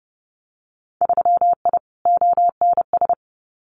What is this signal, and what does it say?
Morse code, heard as a keyed tone, about 30 wpm, 700 Hz. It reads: 3I ONS
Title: Winter Field Day Exchanges Encore 30wpm